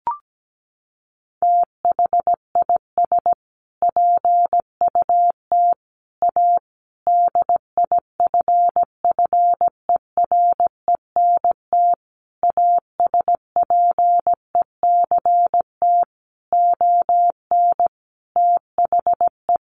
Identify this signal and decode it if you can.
Morse code, heard as a keyed tone, about 17 wpm, 700 Hz. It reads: THIS PUT A DIFFERENT ASPECT ON THE
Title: Sentences from Top 1300 Words 17wpm